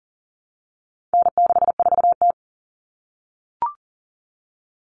Morse code, heard as a keyed tone, about 40 wpm, 700 Hz. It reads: N64T
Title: Common Cut Numbers - 4-Digit 40wpm